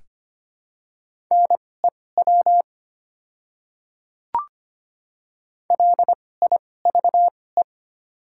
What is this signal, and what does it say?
Morse code, heard as a keyed tone, about 25 wpm, spaced as for 18 wpm, 700 Hz. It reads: NEW LIVE
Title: 2x Character Spacing - Top 200 Words - 25wpm